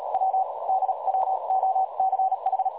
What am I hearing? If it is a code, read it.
Morse code, heard as a keyed tone, about 38 wpm, spaced as for 32 wpm, 760 Hz, with noise. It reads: W DWOCH
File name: dlazaawansowanych / nagraniazpasma / websdr recording 3565.0kHz